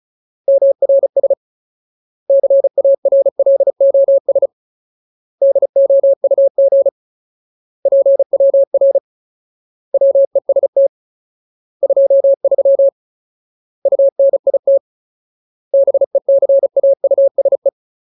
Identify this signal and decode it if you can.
Morse code, heard as a keyed tone, about 34 wpm, 550 Hz. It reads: MRS CARLOS DOUG PWR WEST 23 UNIT BECAUSE
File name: Ultimate Word List / 35WPM Ultimate